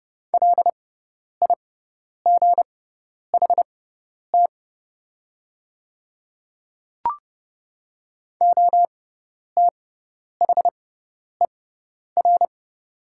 Morse code, heard as a keyed tone, about 30 wpm, spaced as for 10 wpm, 700 Hz. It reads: LIGHT OTHER
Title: 6x Character Spacing - Top 200 Words - 30wpm